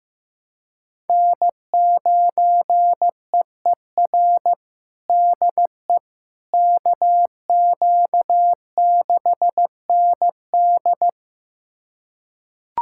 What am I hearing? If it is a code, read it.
Morse code, heard as a keyed tone, about 15 wpm, 700 Hz. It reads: N9EER DE KQ6ND
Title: USA Callsign DE Callsign - 5 Characters - 15wpm